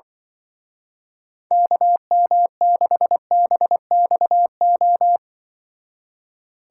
Morse code, heard as a keyed tone, about 25 wpm, 700 Hz. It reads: KM6BXO
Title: Speed Racing - USA 2x3 Call Signs 20wpm